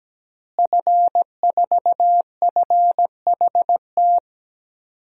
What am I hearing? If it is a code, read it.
Morse code, heard as a keyed tone, about 17 wpm, 700 Hz. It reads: F4FHT
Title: DXCC-100 Call Signs - 5 Characters - Repeated 2x - 17wpm